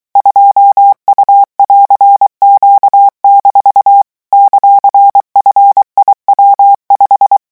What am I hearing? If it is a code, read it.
Morse code, heard as a keyed tone, about 23 wpm, 800 Hz. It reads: 2U<AR>Q- ;FIW5
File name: morse mp3 / revision19-vitesse 22 mots